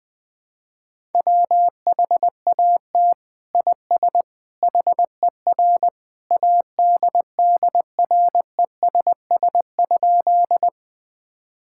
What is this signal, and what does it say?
Morse code, heard as a keyed tone, about 20 wpm, 700 Hz. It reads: WHAT IS HER ADDRESS?
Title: Sentences from Top 1000 Words - Review Part 6 - 20wpm